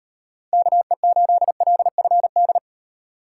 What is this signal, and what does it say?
Morse code, heard as a keyed tone, about 33 wpm, 700 Hz. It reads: KE8LFD